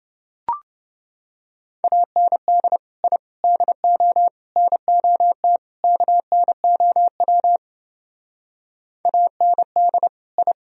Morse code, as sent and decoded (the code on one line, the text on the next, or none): .- -. -.. / .. / -.. --- / -. --- - / -.- -. --- .-- / .- -. -.. / ..
AND I DO NOT KNOW AND I